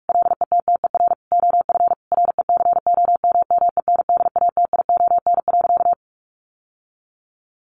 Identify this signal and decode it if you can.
Morse code, heard as a keyed tone, about 44 wpm, 700 Hz. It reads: LETTER OF RECOMMENDATION.
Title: Sentences from Top 1400 Words 45wpm